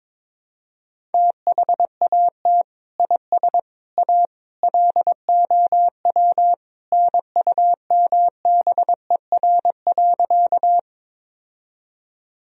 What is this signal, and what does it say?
Morse code, heard as a keyed tone, about 22 wpm, 700 Hz. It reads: THAT IS A LOW NUMBER.
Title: Sentences from Top 1000 Words - Review Part 1 - 22wpm